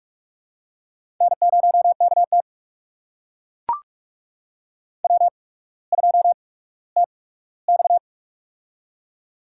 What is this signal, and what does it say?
Morse code, heard as a keyed tone, about 44 wpm, spaced as for 19 wpm, 690 Hz. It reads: N0KT W2TX
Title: USA Call Signs - 4 Characters - 8x Character Spacing - 45wpm